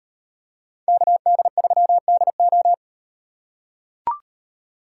Morse code, heard as a keyed tone, about 38 wpm, 700 Hz. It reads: KD3DO